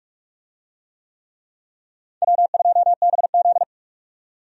Morse code, heard as a keyed tone, about 43 wpm, 700 Hz. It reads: W2BZ
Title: USA Call Signs - 4 Characters - 45wpm